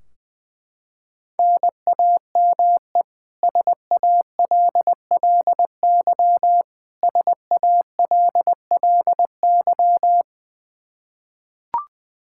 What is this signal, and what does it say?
Morse code, heard as a keyed tone, about 20 wpm, 700 Hz. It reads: NAME SALLY SALLY
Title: K1USN SST QSO Practice 20wpm